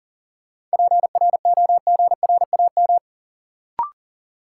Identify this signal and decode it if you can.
Morse code, heard as a keyed tone, about 40 wpm, 700 Hz. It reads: PROGRAM
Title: Sentences from Top 1000 Words - Review Part 4 - New Word Drill - 40wpm